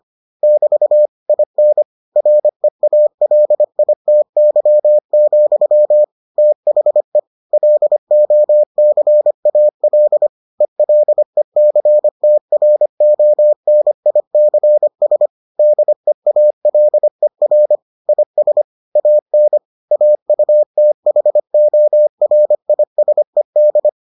Morse code, heard as a keyed tone, about 25 wpm, 600 Hz. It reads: <BT> IN REALITY, THE LOCAL ELECTRONICS DEALER IS AN AUTHORISED